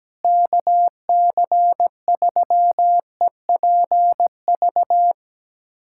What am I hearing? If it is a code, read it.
Morse code, heard as a keyed tone, about 17 wpm, 700 Hz. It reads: KC3EPV
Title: USA Callsign DE Callsign - 6 Characters - 17wpm